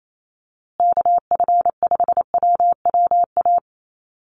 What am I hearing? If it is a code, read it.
Morse code, heard as a keyed tone, about 28 wpm, 700 Hz. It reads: KF5WWA
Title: USA Call Signs Repeated 3x 28wpm